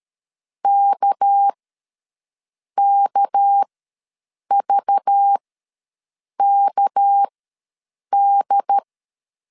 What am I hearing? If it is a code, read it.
Morse code, heard as a keyed tone, about 13 wpm, spaced as for 6 wpm, 800 Hz. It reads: KKVKD
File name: LEZIONI / 6 PKX0 / 6 PK ws 12wpm 3116